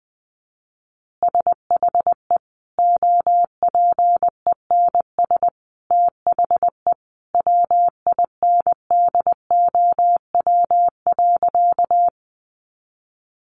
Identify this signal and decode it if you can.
Morse code, heard as a keyed tone, about 20 wpm, 700 Hz. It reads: SHE OPENS THE WINDOW.
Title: Additional Easy Phrases 20wpm